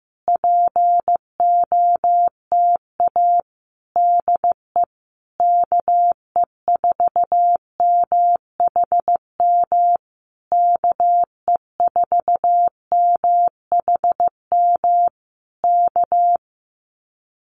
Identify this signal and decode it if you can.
Morse code, heard as a keyed tone, about 15 wpm, 700 Hz. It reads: POTA DE KE4MHM KE4MHM K
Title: Common QSO Phrases Encore 15wpm